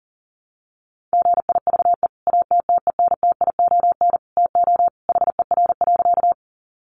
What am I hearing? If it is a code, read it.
Morse code, heard as a keyed tone, about 40 wpm, 700 Hz. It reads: GIVE ATTENTION TO HER.
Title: Sentences from Top 1000 Words 40wpm